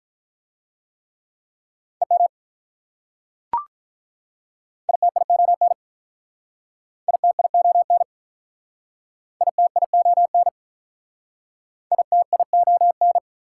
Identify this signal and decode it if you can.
Morse code, heard as a keyed tone, about 45 wpm, 690 Hz. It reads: EM ITION ITION ITION ITION